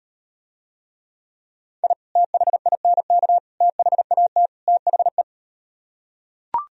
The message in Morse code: I THINK THAT THE

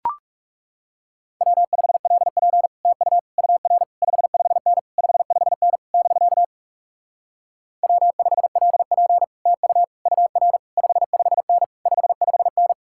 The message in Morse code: W5LP TU UR 55N 55N <BK> W5LP TU UR 55N 55N

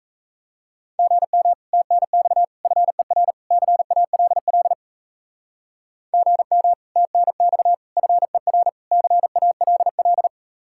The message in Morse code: GM TNX FER CALL GM TNX FER CALL

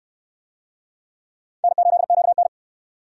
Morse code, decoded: N8QN

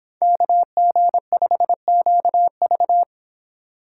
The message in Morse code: KG5QV

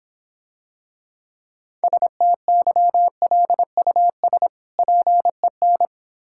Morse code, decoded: STYLUS PEN

